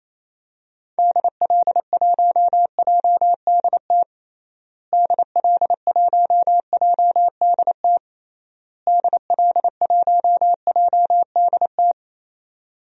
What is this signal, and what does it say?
Morse code, heard as a keyed tone, about 28 wpm, 700 Hz. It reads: DL1JDT DL1JDT DL1JDT
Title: DXCC-100 Call Signs - 6 Characters - Repeated 3x - 28wpm